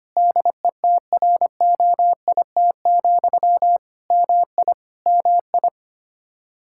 Morse code, heard as a keyed tone, about 25 wpm, 700 Hz. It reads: DETROIT, MI MI
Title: Practice QSO - Exchange 1 - Beginning - 25wpm